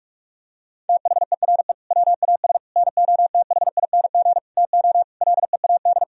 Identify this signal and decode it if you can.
Morse code, heard as a keyed tone, about 46 wpm, 700 Hz. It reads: THERE WAS NOTHING TO LEAD
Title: Sentences from Top 1000 Words - Review Part 2 - 45wpm